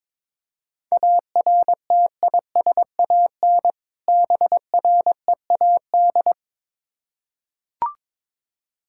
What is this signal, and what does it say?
Morse code, heard as a keyed tone, about 22 wpm, 690 Hz. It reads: ARTISAN BREAD
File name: Difficult Words with Hints - 22wpm